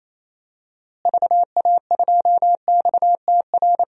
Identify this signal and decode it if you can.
Morse code, heard as a keyed tone, about 28 wpm, 700 Hz. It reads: VA2XTR
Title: DXCC-100 Call Signs - 6 Characters - Repeated 3x - 28wpm